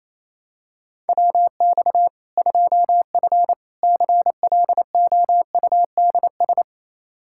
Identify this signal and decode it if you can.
Morse code, heard as a keyed tone, about 28 wpm, 700 Hz. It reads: WX 2F CLOUDS
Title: Practice QSO - Exchange 2 - Info - 28wpm